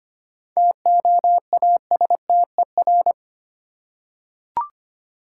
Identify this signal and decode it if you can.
Morse code, heard as a keyed tone, about 25 wpm, 700 Hz. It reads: TOASTER
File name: practice / 25wpm / Easy Words with Hints - 25wpm